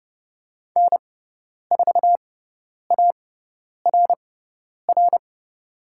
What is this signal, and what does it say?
Morse code, heard as a keyed tone, about 30 wpm, spaced as for 10 wpm, 700 Hz. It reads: N4ARR
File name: Lesson 17g - Letter u - Call Signs 30wpm